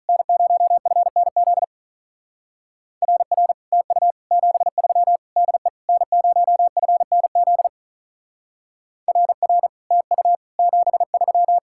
Morse code, heard as a keyed tone, about 44 wpm, 700 Hz. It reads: N0FNZ RR TU 73 DE N0FNZ RR TU 73
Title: Speed-Racing Common QSO Phrases - 35wpm